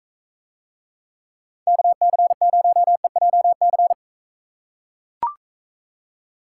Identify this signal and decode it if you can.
Morse code, heard as a keyed tone, about 44 wpm, 700 Hz. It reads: KC0EJC